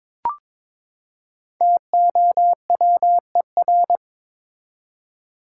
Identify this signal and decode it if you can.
Morse code, heard as a keyed tone, about 22 wpm, 700 Hz. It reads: TOWER
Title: Five Letter Words - 22wpm